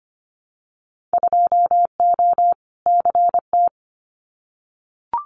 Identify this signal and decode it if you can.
Morse code, heard as a keyed tone, about 25 wpm, 700 Hz. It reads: 2O CT
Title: Winter Field Day Exchanges 25wpm